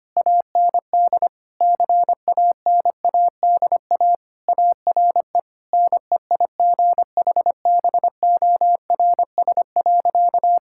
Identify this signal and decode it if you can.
Morse code, heard as a keyed tone, about 25 wpm, 700 Hz. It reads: AND CANADA ARE NEIGHBORS.
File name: practice / 25wpm / Sentences with 4-Word Phrases - 25wpm